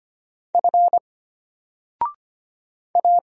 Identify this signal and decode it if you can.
Morse code, heard as a keyed tone, about 25 wpm, 700 Hz. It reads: F A
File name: Single-Letter-Number-25wpm